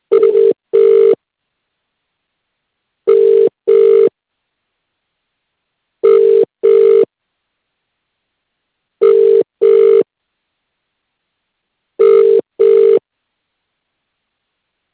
Ringback tone (UK), 400 + 450 Hz.